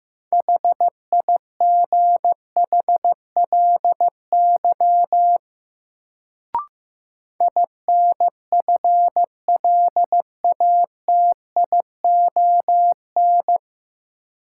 Morse code, heard as a keyed tone, about 15 wpm, 700 Hz. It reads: HIGHLY INFLATION
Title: Sentences from Top 1100 Words - New Word Drill Alt - 15wpm